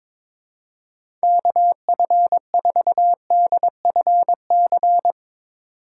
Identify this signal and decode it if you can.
Morse code, heard as a keyed tone, about 22 wpm, 700 Hz. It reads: KF4DFC
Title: USA Call Signs - 6 Characters - 2x Character Spacing - 22wpm